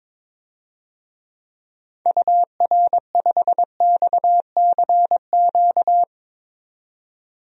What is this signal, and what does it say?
Morse code, heard as a keyed tone, about 22 wpm, 700 Hz. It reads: UR5XCQ
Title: DXCC-100 Call Signs - 6 Characters - 22wpm